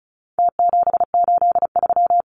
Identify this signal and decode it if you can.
Morse code, heard as a keyed tone, about 34 wpm, 700 Hz. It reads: T783
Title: Numbers - Contest Style - 4-Digit 35wpm